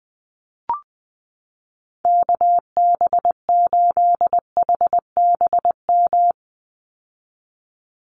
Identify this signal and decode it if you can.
Morse code, heard as a keyed tone, about 20 wpm, 700 Hz. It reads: KB8HBM
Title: USA Call Signs 20wpm